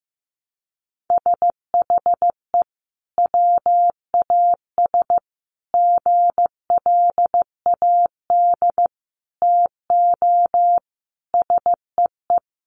Morse code, heard as a keyed tone, about 15 wpm, 700 Hz. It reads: SHE WAS GLAD TO SEE
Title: Sentences from Top 800 Words 15wpm